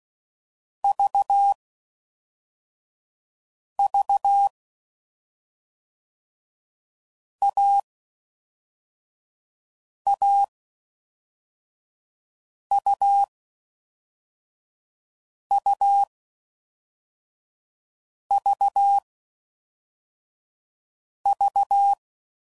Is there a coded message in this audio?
Yes: Morse code from 0.8 to 21.9 s, heard as a keyed tone, about 16 wpm, spaced as for 4 wpm, 800 Hz. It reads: VV AAUUVV